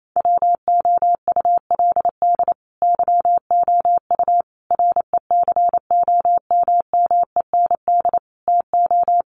Morse code, heard as a keyed tone, about 28 wpm, 700 Hz. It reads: WOULD YOU RECOMMEND TO
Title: Sentences from Top 1200 Words 28wpm